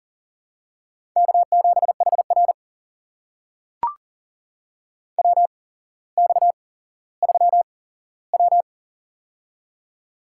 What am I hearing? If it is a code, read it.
Morse code, heard as a keyed tone, about 40 wpm, spaced as for 17 wpm, 700 Hz. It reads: K7HR WX3W
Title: USA Call Signs - 4 Characters - 8x Character Spacing - 40wpm